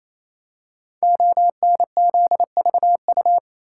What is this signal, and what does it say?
Morse code, heard as a keyed tone, about 28 wpm, 700 Hz. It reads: ONZVU
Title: Code Groups - 5 Letter:Numbers - 28wpm